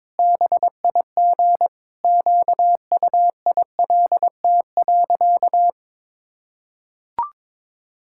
Morse code, heard as a keyed tone, about 22 wpm, 700 Hz. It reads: BIG QUILT.
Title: Pangram Challenge v2 - 22wpm